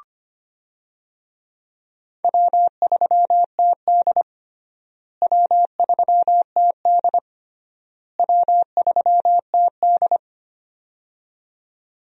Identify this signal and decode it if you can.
Morse code, heard as a keyed tone, about 25 wpm, 700 Hz. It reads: W3TD W3TD W3TD
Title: USA Call Signs - 4 Characters - Repeated 3x - 25wpm